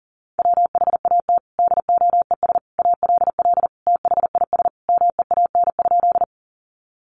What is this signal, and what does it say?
Morse code, heard as a keyed tone, about 40 wpm, 700 Hz. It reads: WHAT DOES ALL THIS MEAN?
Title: Sentences from Top 200 Words 40wpm